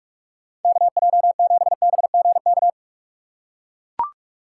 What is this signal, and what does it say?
Morse code, heard as a keyed tone, about 44 wpm, 700 Hz. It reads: KJ7BGK
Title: USA Call Signs - 6 Characters - 3x Character Spacing - 45wpm